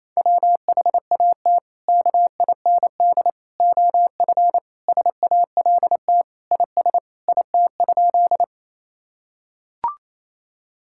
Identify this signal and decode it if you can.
Morse code, heard as a keyed tone, about 27 wpm, 700 Hz. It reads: WHAT KIND OF SALT IS IT?
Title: Sentences from Top 1000 Words - Review Part 5 - 28wpm